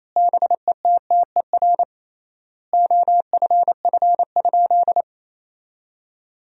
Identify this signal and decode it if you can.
Morse code, heard as a keyed tone, about 28 wpm, 700 Hz. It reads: BETTER OFF?
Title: 3x Word-Spacing - Sentences from Top 500 Words - 28wpm